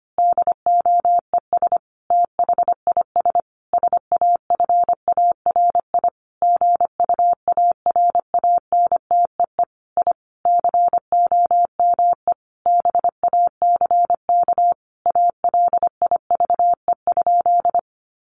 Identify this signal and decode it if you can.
Morse code, heard as a keyed tone, about 25 wpm, 700 Hz. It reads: DOES THIS SAFARI GUARANTEE I COME BACK ALIVE?
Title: A Sound Of Thunder 25wpm